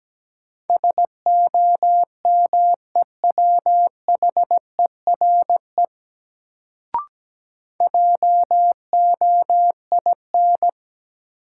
Morse code, heard as a keyed tone, about 17 wpm, 700 Hz. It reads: SOMEWHERE JOIN
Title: Sentences from Top 1400 Words - New Word Drill Alt - 17wpm